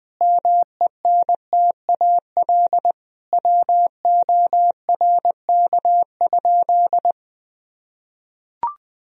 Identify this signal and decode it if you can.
Morse code, heard as a keyed tone, about 20 wpm, 700 Hz. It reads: MENTAL WORK?
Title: Sentences from Top 1000 Words - Review Part 4 - 20wpm